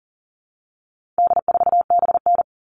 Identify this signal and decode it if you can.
Morse code, heard as a keyed tone, about 40 wpm, 700 Hz. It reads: D4BN
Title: All Cut Numbers - 4-Digit 40wpm